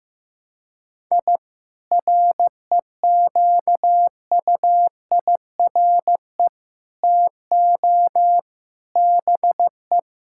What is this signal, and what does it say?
Morse code, heard as a keyed tone, about 15 wpm, 700 Hz. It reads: I REQUIRE TO BE
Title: Sentences from Top 1000 Words - Review Part 6 - 15wpm